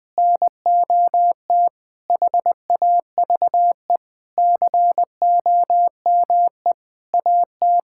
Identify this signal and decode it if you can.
Morse code, heard as a keyed tone, about 21 wpm, 700 Hz. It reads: NOT HAVE COME AT
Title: Sentences with 8-Word Phrases - 20wpm